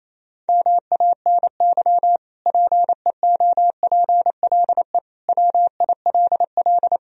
Morse code, heard as a keyed tone, about 28 wpm, 700 Hz. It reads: MANY PEOPLE WILL
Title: Sentences from Top 1000 Words - Review Part 1 - 28wpm